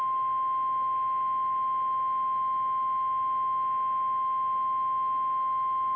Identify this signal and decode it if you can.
Emergency Alert System (SAME) tones (the 1050 Hz attention tone). It is heard from 1.0 to 5.3 s.
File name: eas / KKPRKXPN / KKPRKXPN